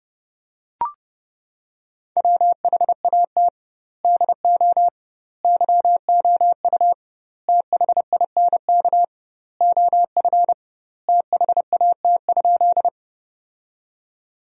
Morse code, heard as a keyed tone, about 30 wpm, 700 Hz. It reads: WHAT DO YOU THINK OF THAT?
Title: 2x Word-Spacing - Sentences from Top 200 Words - 30wpm